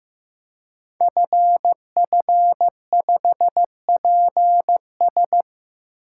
Morse code, heard as a keyed tone, about 15 wpm, 700 Hz. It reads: FF5PS
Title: Code Groups - 5 Letter:Numbers - 15wpm